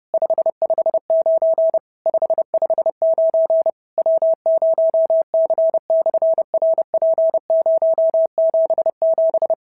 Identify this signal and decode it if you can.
Morse code, heard as a keyed tone, about 30 wpm, 650 Hz. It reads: 559 559 W0C/RP077
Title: SOTA QSO Practice 30wpm